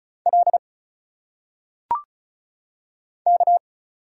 Morse code, heard as a keyed tone, about 36 wpm, 700 Hz. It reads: L K
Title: Single Letters - Phonetic Pronunciation 35wpm